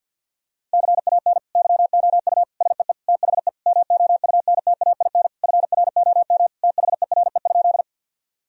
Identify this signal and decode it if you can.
Morse code, heard as a keyed tone, about 46 wpm, 700 Hz. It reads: CAN YOU SEE THE MOUNTAIN FROM THERE?
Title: Sentences from Top 1000 Words - Review Part 1 - 50wpm